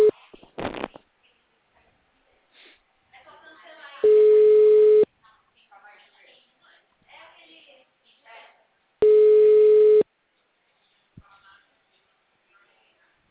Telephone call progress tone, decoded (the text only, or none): ringback tone